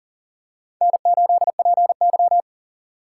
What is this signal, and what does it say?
Morse code, heard as a keyed tone, about 37 wpm, 700 Hz. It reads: N8PY
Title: USA Call Signs - 4 Characters - 3x Character Spacing - 40wpm